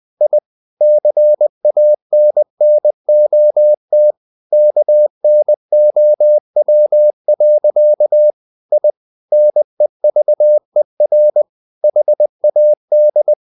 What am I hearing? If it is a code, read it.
Morse code, heard as a keyed tone, about 20 wpm, 600 Hz. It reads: I CANNOT KNOW. I NEVER HAD
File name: the old man and the sea part 2 20wpm